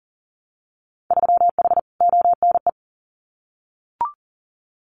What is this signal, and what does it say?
Morse code, heard as a keyed tone, about 41 wpm, 700 Hz. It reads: 3H ONE
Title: Winter Field Day Exchanges Encore 40wpm